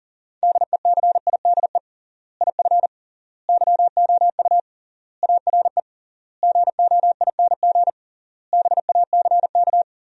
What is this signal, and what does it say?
Morse code, heard as a keyed tone, about 40 wpm, 700 Hz. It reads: DECIDE IF YOU ARE GOING BACK
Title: 3x Word-Spacing - Sentences from Top 500 Words - 40wpm